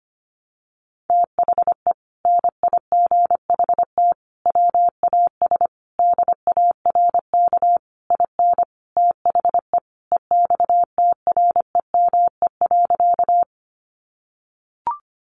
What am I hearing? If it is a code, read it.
Morse code, heard as a keyed tone, about 25 wpm, 700 Hz. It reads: THE NIGHT WAS DARK IN THE EXTREME.
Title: Sentences from Top 1300 Words 25wpm